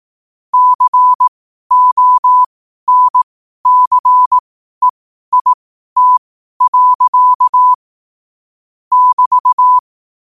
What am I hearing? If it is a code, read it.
Morse code, heard as a keyed tone, about 18 wpm, spaced as for 12 wpm, 1000 Hz. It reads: CONCEIT. <BT>
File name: qotdfiles / 12wpm